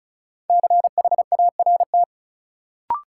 Morse code, heard as a keyed tone, about 36 wpm, 700 Hz. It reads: CHART
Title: Sentences from Top 1300 Words - New Word Drill - 35wpm